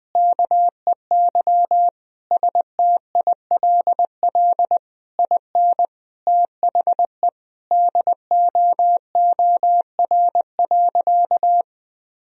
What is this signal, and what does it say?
Morse code, heard as a keyed tone, about 21 wpm, 700 Hz. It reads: KEY STILL IN THE DOOR.
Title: Sentences from Top 1000 Words - Review Part 4 - 20wpm